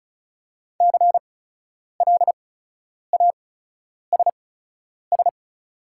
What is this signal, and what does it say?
Morse code, heard as a keyed tone, about 36 wpm, spaced as for 10 wpm, 700 Hz. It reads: CLASS
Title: 8x Character Spacing - Top 500 Words - 35wpm